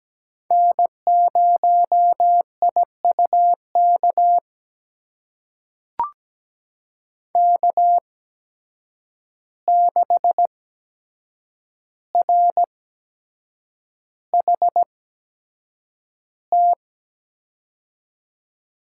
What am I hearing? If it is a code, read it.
Morse code, heard as a keyed tone, about 17 wpm, spaced as for 7 wpm, 700 Hz. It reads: N0IUK K6RHT
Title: USA Call Signs - 5 Characters - 8x Character Spacing - 17wpm